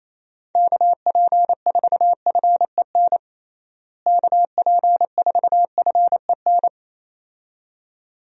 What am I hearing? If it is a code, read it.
Morse code, heard as a keyed tone, about 28 wpm, 700 Hz. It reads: KP4FEN KP4FEN
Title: DXCC-100 Call Signs - 6 Characters - Repeated 2x - 28wpm